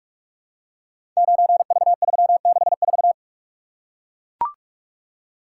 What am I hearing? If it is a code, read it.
Morse code, heard as a keyed tone, about 44 wpm, 700 Hz. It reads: 9V364
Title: Code Groups - 5 Letter:Numbers - 45wpm